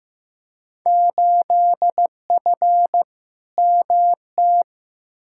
Morse code, heard as a keyed tone, about 15 wpm, 700 Hz. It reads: 8F MT
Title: Field Day Exchanges Encore - 15wpm